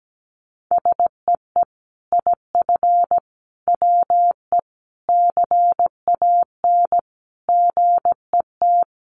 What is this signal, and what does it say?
Morse code, heard as a keyed tone, about 17 wpm, 700 Hz. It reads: SEE IF WE CAN GET
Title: Sets of 5 Words Encore2 17wpm